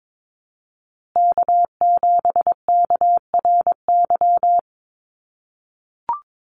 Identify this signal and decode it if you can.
Morse code, heard as a keyed tone, about 22 wpm, 700 Hz. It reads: K7KRY
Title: USA Call Signs Repeated 2x 22wpm